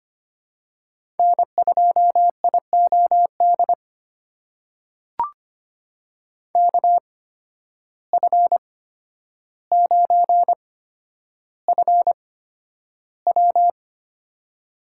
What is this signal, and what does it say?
Morse code, heard as a keyed tone, about 25 wpm, spaced as for 11 wpm, 700 Hz. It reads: N2IOD KF9FW
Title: USA Call Signs - 5 Characters - 8x Character Spacing - 25wpm